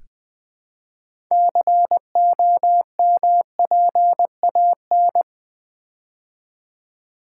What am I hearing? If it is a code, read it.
Morse code, heard as a keyed tone, about 20 wpm, 700 Hz. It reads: COMPAN
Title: Build A Word - 3 Letters At A Time - 9-Letter Words - 20wpm